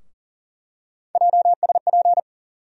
Morse code, heard as a keyed tone, about 40 wpm, 700 Hz. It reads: JSP
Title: USA Call Sign Postfixes - 3 Letters 40wpm